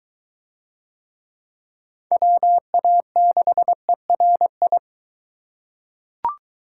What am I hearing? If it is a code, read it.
Morse code, heard as a keyed tone, about 20 wpm, 700 Hz. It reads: WA6ERI